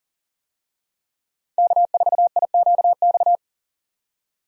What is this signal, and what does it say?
Morse code, heard as a keyed tone, about 40 wpm, 700 Hz. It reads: K4IQX
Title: USA Call Signs - 5 Characters - 40wpm